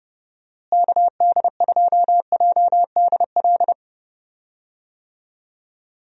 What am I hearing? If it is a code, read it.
Morse code, heard as a keyed tone, about 30 wpm, 700 Hz. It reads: KD2JDL
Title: USA Call Signs - 6 Characters - Repeated 3x - 30wpm